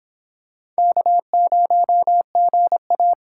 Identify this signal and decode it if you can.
Morse code, heard as a keyed tone, about 25 wpm, 700 Hz. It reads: K0GA